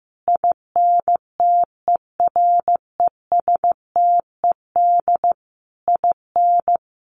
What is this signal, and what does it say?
Morse code, heard as a keyed tone, about 15 wpm, 700 Hz. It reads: INTERESTED IN
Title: Sentences from Top 1200 Words 15wpm